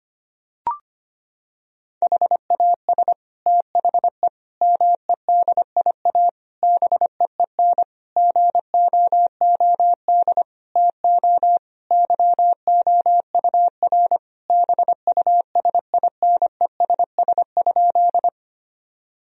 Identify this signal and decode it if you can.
Morse code, heard as a keyed tone, about 25 wpm, 700 Hz. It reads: HAS THE MEDIA BEEN GOOD TO YOUR BUSINESS?
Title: Sentences from Top 1000 Words - Review Part 4 - 25wpm